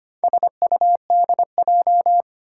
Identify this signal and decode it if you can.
Morse code, heard as a keyed tone, about 25 wpm, 700 Hz. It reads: SUDJ
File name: practice / 25wpm / Four Letter-Number Combinations - 25wpm